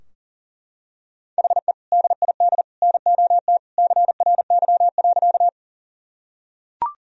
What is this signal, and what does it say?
Morse code, heard as a keyed tone, about 37 wpm, 700 Hz. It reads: HE DID NOT CRY.
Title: Sentences from Top 1000 Words - Review Part 2 - 40wpm